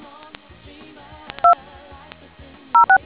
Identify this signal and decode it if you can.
DTMF, heard as telephone touch tones, tones of about 110 ms. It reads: 206